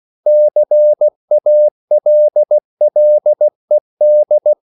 Morse code, heard as a keyed tone, about 16 wpm, 600 Hz. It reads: CALLED
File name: CW / Practice / EnglishQuotations0116